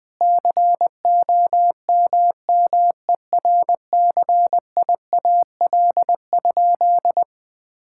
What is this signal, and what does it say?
Morse code, heard as a keyed tone, about 20 wpm, 700 Hz. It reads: COMMERCIAL?
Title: Sentences from Top 1000 Words - Review Part 5 - 20wpm